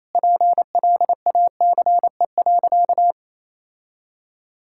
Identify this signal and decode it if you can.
Morse code, heard as a keyed tone, about 28 wpm, 700 Hz. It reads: PLACE.
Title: Sentences from Top 1600 Words 28wpm